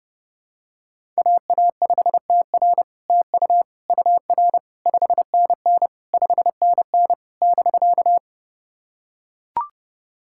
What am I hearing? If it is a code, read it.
Morse code, heard as a keyed tone, about 30 wpm, 700 Hz. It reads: AA5TR TU UR 5NN 5NN <BK>
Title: SOTA QSO Practice 30wpm